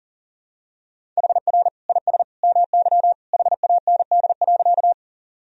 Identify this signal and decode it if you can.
Morse code, heard as a keyed tone, about 40 wpm, 700 Hz. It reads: HR IS MY HAND.
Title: Sentences from Top 200 Words and Abbreviations 40wpm